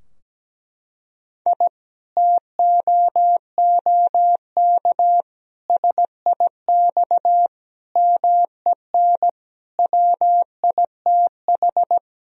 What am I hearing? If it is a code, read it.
Morse code, heard as a keyed tone, about 17 wpm, 700 Hz. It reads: I TOOK SIX MEN WITH
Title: Sentences from Top 1000 Words - Review Part 2 - 17wpm